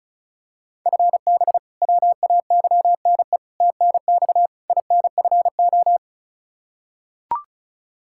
Morse code, heard as a keyed tone, about 34 wpm, 700 Hz. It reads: FB WAYNE TNX INFO
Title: K1USN SST QSO Practice 35wpm